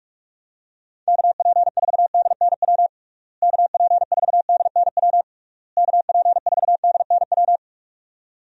Morse code, heard as a keyed tone, about 46 wpm, 700 Hz. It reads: KP4DNW KP4DNW KP4DNW
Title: USA Call Signs - 6 Characters - Repeated 3x - 45wpm